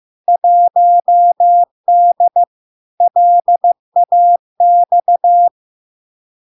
Morse code, heard as a keyed tone, about 15 wpm, 700 Hz. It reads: 1D LAX
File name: LARK-FD-2022-Log-in-CW-15wpm-700-Hz